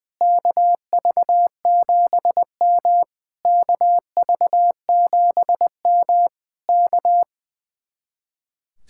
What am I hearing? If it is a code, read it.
Morse code, heard as a keyed tone, about 21 wpm, 700 Hz. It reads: KV7M KV7M K
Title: Speed-Racing Common QSO Phrases - 20wpm